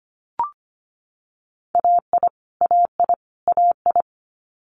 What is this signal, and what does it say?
Morse code, heard as a keyed tone, about 25 wpm, 700 Hz. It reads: AI AI AI
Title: Multi-letter Phonemes Repeated 3x 25wpm